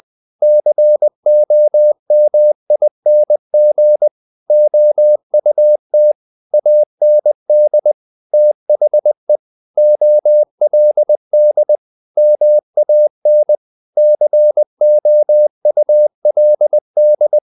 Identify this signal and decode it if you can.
Morse code, heard as a keyed tone, about 20 wpm, 600 Hz. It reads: COMING OUT AND THE OLD MAN COULD
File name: the old man and the sea part 2 20wpm